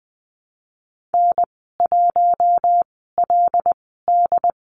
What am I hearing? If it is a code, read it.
Morse code, heard as a keyed tone, about 20 wpm, spaced as for 14 wpm, 700 Hz. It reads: N1LD
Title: 082 Lesson 14g - Number 2 - Call Signs 20wpm